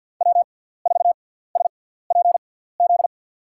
Morse code, heard as a keyed tone, about 53 wpm, spaced as for 17 wpm, 700 Hz. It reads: W4SPZ
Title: USA Call Signs - 5 Characters - 6x Character Spacing - 50wpm